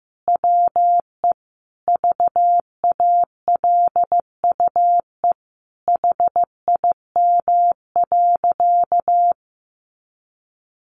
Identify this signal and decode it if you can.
Morse code, heard as a keyed tone, about 14 wpm, 700 Hz. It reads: WE VALUE HIM.
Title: Sentences from Top 600 Words 15wpm